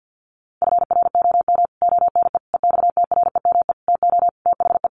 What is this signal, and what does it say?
Morse code, heard as a keyed tone, about 50 wpm, 700 Hz. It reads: FROM ONE EXTREME TO THE
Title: Sentences from Top 1300 Words 50wpm